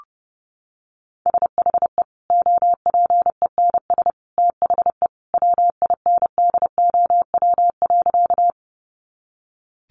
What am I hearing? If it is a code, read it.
Morse code, heard as a keyed tone, about 28 wpm, 700 Hz. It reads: SHE OPENS THE WINDOW.
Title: Additional Easy Phrases 30wpm